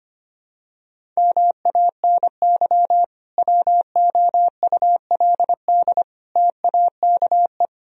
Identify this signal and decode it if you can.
Morse code, heard as a keyed tone, about 25 wpm, 700 Hz. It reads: MANY WOULD TAKE
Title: Sentences from Top 1000 Words - Review Part 1 - 25wpm